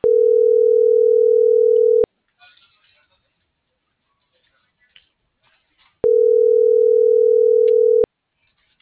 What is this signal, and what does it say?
Ringback tone (North America), 440 + 480 Hz.